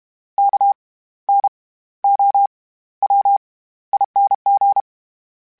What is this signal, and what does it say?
Morse code, heard as a keyed tone, about 32 wpm, spaced as for 16 wpm, 800 Hz. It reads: KNOWING
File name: Special INGs / Special INGs-mp3s / SlowToFaster ing @30